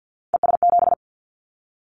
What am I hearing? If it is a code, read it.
Morse code, heard as a keyed tone, about 50 wpm, 700 Hz. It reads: ES7
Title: Callsign Prefixes for Top 50 DXCC - 50wpm